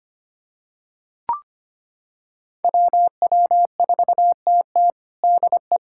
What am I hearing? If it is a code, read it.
Morse code, heard as a keyed tone, about 25 wpm, 700 Hz. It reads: WW4TT DE
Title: USA Callsign DE Callsign - 5 Characters - 25wpm